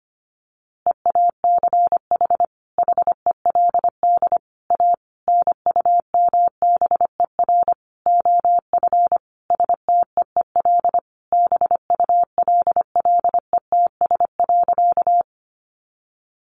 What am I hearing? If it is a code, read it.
Morse code, heard as a keyed tone, about 25 wpm, 700 Hz. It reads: EACH HELD A NUMBER OF STEEL BULLETS.
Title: A Sound Of Thunder 25wpm